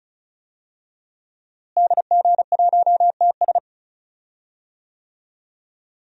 Morse code, heard as a keyed tone, about 35 wpm, 700 Hz. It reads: DG1TS
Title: DXCC-100 Call Signs - 5 Characters - Repeated 2x - 35wpm